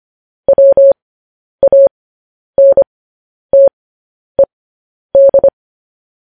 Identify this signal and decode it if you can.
Morse code, heard as a keyed tone, about 25 wpm, spaced as for 10 wpm, 550 Hz. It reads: WANTED
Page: News Headlines in Morse Code at 10 WPM